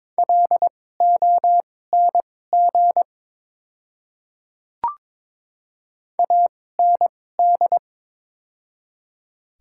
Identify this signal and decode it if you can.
Morse code, heard as a keyed tone, about 23 wpm, spaced as for 16 wpm, 700 Hz. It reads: LONG AND
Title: 2x Character Spacing - Top 100 Words - 22wpm